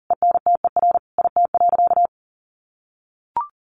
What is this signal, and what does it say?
Morse code, heard as a keyed tone, about 40 wpm, 700 Hz. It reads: ENTER IT.
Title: Sentences from Top 1300 Words 40wpm